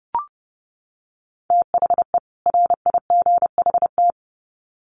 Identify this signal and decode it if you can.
Morse code, heard as a keyed tone, about 30 wpm, 700 Hz. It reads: THE RIGHT
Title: Sets of 2 Words 30wpm